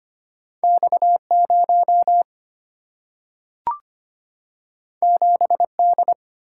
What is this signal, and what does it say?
Morse code, heard as a keyed tone, about 25 wpm, 700 Hz. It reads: X 0 7 D
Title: Single-Letter Single-Number 25wpm